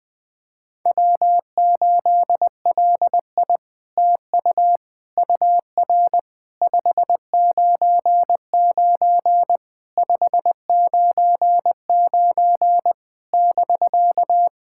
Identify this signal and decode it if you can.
Morse code, heard as a keyed tone, about 21 wpm, 700 Hz. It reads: W8LI TU UR 599 599 <BK>
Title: SOTA QSO Practice 20wpm